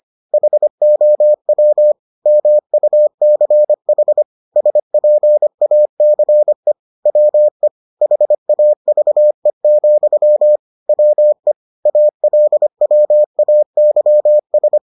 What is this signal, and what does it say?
Morse code, heard as a keyed tone, about 25 wpm, 600 Hz. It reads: HOW MUCH SPACE WE HAVE, WE ALWAYS